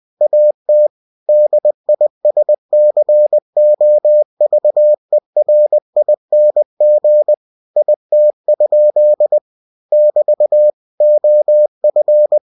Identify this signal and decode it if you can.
Morse code, heard as a keyed tone, about 20 wpm, 600 Hz. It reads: AT DISCOVERING IT? <BT> OF
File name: cw-books / winnie-the-pooh / winnie-20wpm-page-0077